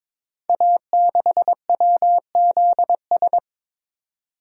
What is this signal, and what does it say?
Morse code, heard as a keyed tone, about 22 wpm, 700 Hz. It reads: A6WZS
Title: Code Groups - 5 Letter:Numbers - 22wpm